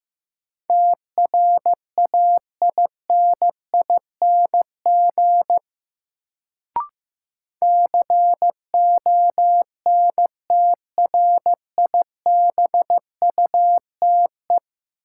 Morse code, heard as a keyed tone, about 16 wpm, 700 Hz. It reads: TRAINING CONTRIBUTE
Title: Sentences from Top 1400 Words - New Word Drill Alt - 15wpm